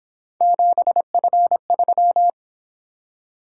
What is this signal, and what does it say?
Morse code, heard as a keyed tone, about 26 wpm, 700 Hz. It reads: 7F3
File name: Speed-Racing - Three Letter-Number Combinations - 22wpm